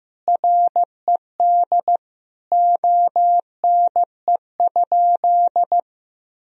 Morse code, heard as a keyed tone, about 16 wpm, 700 Hz. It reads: RED ONE?
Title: Sentences from Top 1000 Words - Review Part 2 - 15wpm